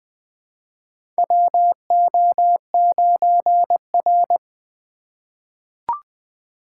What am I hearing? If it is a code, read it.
Morse code, heard as a keyed tone, about 20 wpm, 700 Hz. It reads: WO9R
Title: USA Call Signs - 3x Character Spacing - 20wpm